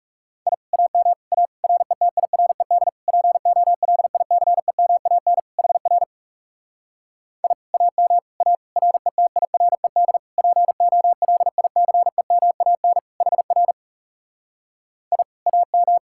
Morse code, heard as a keyed tone, about 40 wpm, 700 Hz. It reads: I AM A RETIRED POLICEMAN HR I AM A RETIRED POLICEMAN HR I AM